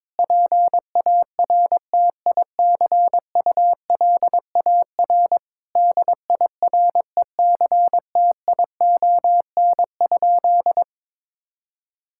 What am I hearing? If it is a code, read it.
Morse code, heard as a keyed tone, about 22 wpm, 700 Hz. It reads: PARTICULAR DIRECTION?
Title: Sentences from Top 1000 Words - Review Part 5 - 22wpm